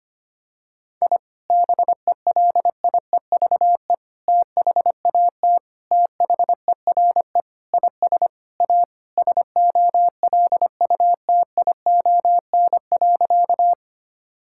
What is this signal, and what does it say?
Morse code, heard as a keyed tone, about 25 wpm, 700 Hz. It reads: I BELIEVE THAT THERE IS A SOLUTION.
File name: practice / 25wpm / Sentences with 4-Word Phrases - 25wpm